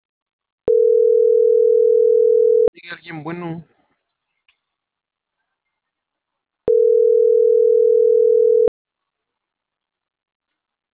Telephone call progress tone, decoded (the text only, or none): ringback tone